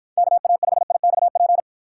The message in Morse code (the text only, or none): KA5IXP